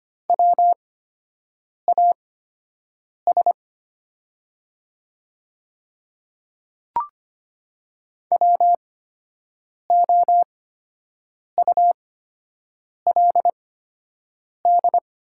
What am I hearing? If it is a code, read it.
Morse code, heard as a keyed tone, about 25 wpm, spaced as for 7 wpm, 700 Hz. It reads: WAS WOULD
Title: 8x Character Spacing - Top 100 Words - 25wpm